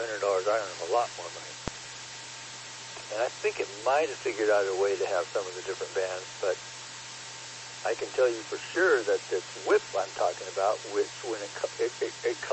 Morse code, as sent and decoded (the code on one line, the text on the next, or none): none